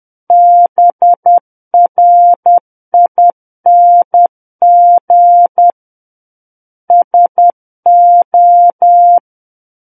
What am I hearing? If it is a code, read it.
Morse code, heard as a keyed tone, about 10 wpm, 700 Hz. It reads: BRING SO
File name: Book One / 01-Chapter 1 - 10 WPM